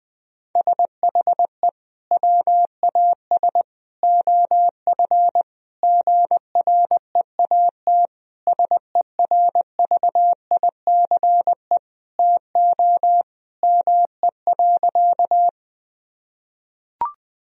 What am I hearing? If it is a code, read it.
Morse code, heard as a keyed tone, about 20 wpm, 700 Hz. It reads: SHE WAS OF GREAT SERVICE TO ME.
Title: Sentences from Top 1000 Words - Review Part 4 - 20wpm